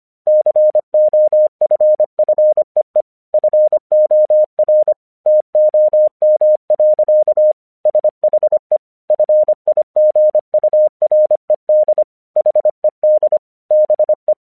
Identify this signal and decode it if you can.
Morse code, heard as a keyed tone, about 25 wpm, 610 Hz. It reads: COFFEE FOR TOM. SHE FIGURED HED BE
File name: 1 English 25wpm / AQC08